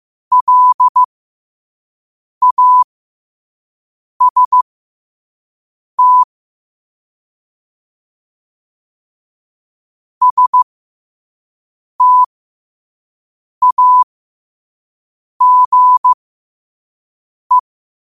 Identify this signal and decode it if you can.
Morse code, heard as a keyed tone, about 15 wpm, spaced as for 5 wpm, 1000 Hz. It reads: LAST STAGE
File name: qotdfiles / 5wpm